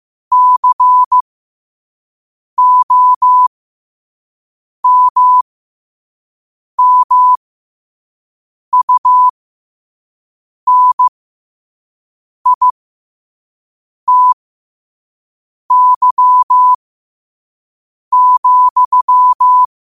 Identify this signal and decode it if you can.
Morse code, heard as a keyed tone, about 15 wpm, spaced as for 5 wpm, 1000 Hz. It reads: COMMUNITY,
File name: qotdfiles / 5wpm